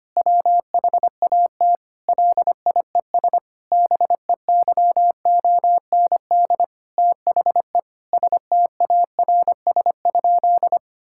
Morse code, heard as a keyed tone, about 25 wpm, 700 Hz. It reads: WHAT LIES BEYOND THE STARS?
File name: practice / 25wpm / Difficult Sentences with Hints - 25wpm